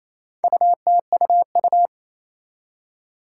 Morse code, heard as a keyed tone, about 27 wpm, 700 Hz. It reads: UTUU
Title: Four Challenging Letter-Number Combinations 28wpm